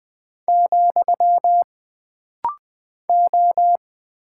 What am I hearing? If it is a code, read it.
Morse code, heard as a keyed tone, about 20 wpm, 700 Hz. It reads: , O
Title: Single Character (Letter:Number:Punctuation:Prosign) - Rapid-Fire - 20wpm